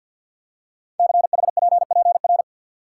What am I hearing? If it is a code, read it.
Morse code, heard as a keyed tone, about 49 wpm, 690 Hz. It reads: CHPPR
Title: Code Groups - 5 Letters - 50wpm